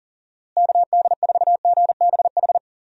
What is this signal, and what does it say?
Morse code, heard as a keyed tone, about 37 wpm, 700 Hz. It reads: KD4GBH
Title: USA Call Signs - 6 Characters - Repeated 2x - 40wpm